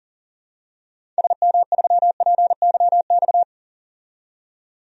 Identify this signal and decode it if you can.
Morse code, heard as a keyed tone, about 40 wpm, 700 Hz. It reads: SM3PYX
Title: DXCC-100 Call Signs - 6 Characters - 40wpm